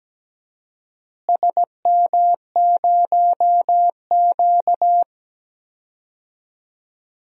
Morse code, heard as a keyed tone, about 17 wpm, 700 Hz. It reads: SM0Q
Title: DXCC-100 Call Signs - 4 Characters - Repeated 3x - 17wpm